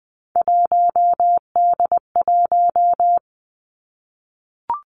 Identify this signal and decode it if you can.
Morse code, heard as a keyed tone, about 20 wpm, 700 Hz. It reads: 1D1
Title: Three Challenging Letter-Number Combinations 20wpm